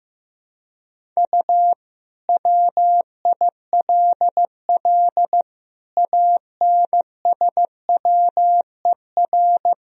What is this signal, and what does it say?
Morse code, heard as a keyed tone, about 16 wpm, 700 Hz. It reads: U WILL ANSWER
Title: Sentences from Top 300 Words and Abbreviations 15wpm